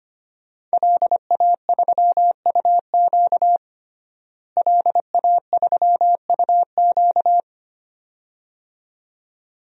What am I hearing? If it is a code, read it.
Morse code, heard as a keyed tone, about 25 wpm, 700 Hz. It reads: LA3UQ LA3UQ
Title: DXCC-100 Call Signs - 5 Characters - Repeated 2x - 25wpm